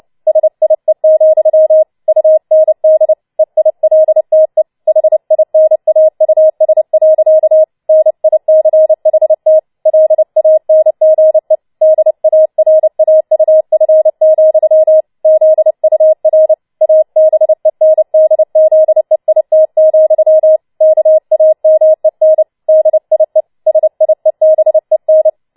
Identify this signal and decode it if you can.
Morse code, heard as a keyed tone, about 29 wpm, 630 Hz. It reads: SIE, UND EILTE HINAUS. NICHT LANGE DARAUF, ZUR ABENDZEIT, KAMEN DIE SIEBEN
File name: Schneewittchen Teil3 30wpm